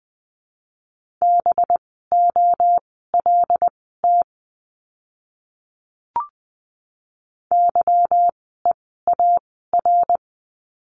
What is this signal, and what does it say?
Morse code, heard as a keyed tone, about 20 wpm, spaced as for 14 wpm, 700 Hz. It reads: BOLT YEAR
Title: 185 Lesson 30f - Letter b - Words 20wpm